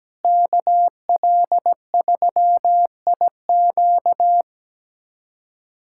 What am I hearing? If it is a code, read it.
Morse code, heard as a keyed tone, about 17 wpm, 700 Hz. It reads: KL3IQ
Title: USA Callsign DE Callsign - 5 Characters - 17wpm